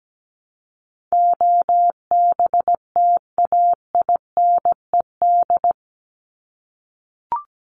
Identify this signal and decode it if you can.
Morse code, heard as a keyed tone, about 17 wpm, 700 Hz. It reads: OBTAINED
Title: ED Suffix - Top 100 Words with Base Words 17wpm